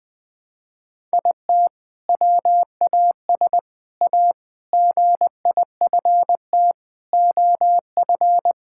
Morse code, heard as a keyed tone, about 20 wpm, 700 Hz. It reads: IT WAS A GIFT OF
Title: Sentences from Top 1000 Words - Review Part 6 - 20wpm